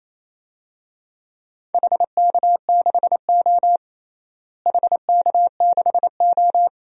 Morse code, heard as a keyed tone, about 28 wpm, 700 Hz. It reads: HK6O HK6O
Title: DXCC-100 Call Signs - 4 Characters - Repeated 2x - 28wpm